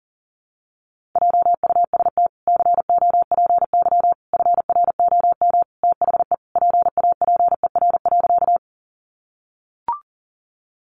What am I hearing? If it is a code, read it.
Morse code, heard as a keyed tone, about 37 wpm, 700 Hz. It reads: JUST COPY FROM THE PAPER.
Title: Sentences from Top 1400 Words 40wpm